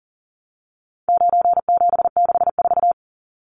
Morse code, heard as a keyed tone, about 40 wpm, 700 Hz. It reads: 9764
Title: Numbers - 4-Digit 40wpm